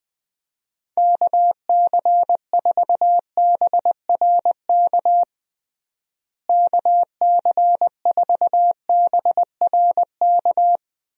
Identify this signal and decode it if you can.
Morse code, heard as a keyed tone, about 19 wpm, 700 Hz. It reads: KC4BRK KC4BRK
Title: USA Call Signs - 6 Characters - Repeated 2x - 20wpm